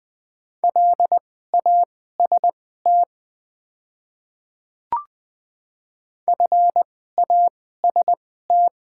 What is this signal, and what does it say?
Morse code, heard as a keyed tone, about 20 wpm, spaced as for 14 wpm, 700 Hz. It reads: LAST FAST
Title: 2x Character Spacing - Top 500 Words - 20wpm